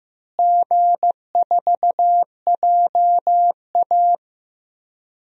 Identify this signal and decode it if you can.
Morse code, heard as a keyed tone, about 15 wpm, 700 Hz. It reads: G4JA
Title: DXCC-100 Call Signs - 4 Characters - Repeated 2x - 15wpm